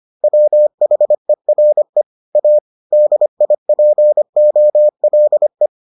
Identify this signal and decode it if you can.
Morse code, heard as a keyed tone, about 25 wpm, 600 Hz. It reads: WHERE A DIPOLE